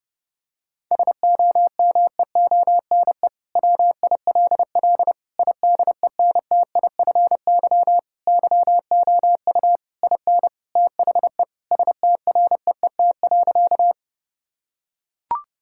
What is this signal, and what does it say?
Morse code, heard as a keyed tone, about 31 wpm, 700 Hz. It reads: SOMEONE WILL IDENTIFY YOU IN THE STREET.
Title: Sentences from Top 1000 Words - Review Part 6 - 30wpm